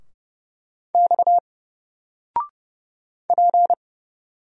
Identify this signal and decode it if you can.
Morse code, heard as a keyed tone, about 29 wpm, 700 Hz. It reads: X P
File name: Single-Letter-Number-30wpm